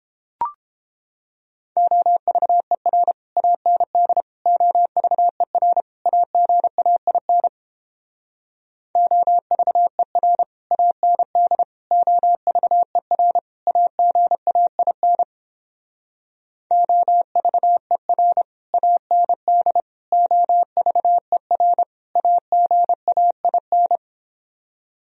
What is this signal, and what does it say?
Morse code, heard as a keyed tone, about 29 wpm, 700 Hz. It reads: OVER AND OVER AGAIN OVER AND OVER AGAIN OVER AND OVER AGAIN